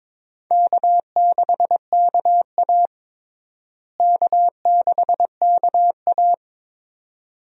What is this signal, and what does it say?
Morse code, heard as a keyed tone, about 21 wpm, 700 Hz. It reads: K6KA K6KA
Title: USA Call Signs - 4 Characters - Repeated 2x - 22wpm